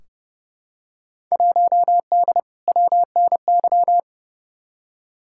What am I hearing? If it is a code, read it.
Morse code, heard as a keyed tone, about 32 wpm, 700 Hz. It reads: 1D WNY
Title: Field Day Exchanges - 30wpm